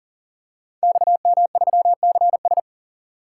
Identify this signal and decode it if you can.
Morse code, heard as a keyed tone, about 39 wpm, 700 Hz. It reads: XM3CS
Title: Code Groups - 5 Letter:Numbers - 40wpm